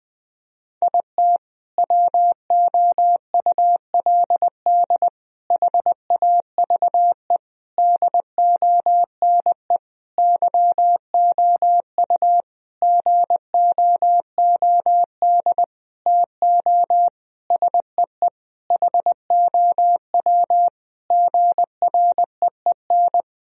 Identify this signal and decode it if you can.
Morse code, heard as a keyed tone, about 20 wpm, 700 Hz. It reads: IT WOULD HAVE DONE YOU GOOD TO SEE HOW GREEN
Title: Sentences from Top 1000 Words - Review Part 2 - 20wpm